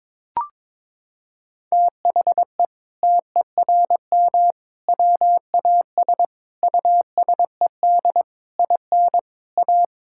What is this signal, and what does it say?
Morse code, heard as a keyed tone, about 22 wpm, 700 Hz. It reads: THE TERM WAS USED IN A
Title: Sentences from Top 1000 Words - Review Part 6 - 22wpm